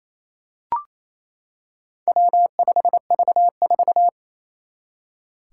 Morse code, heard as a keyed tone, about 29 wpm, 700 Hz. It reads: W5V4
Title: Four Challenging Letter-Number Combinations 28wpm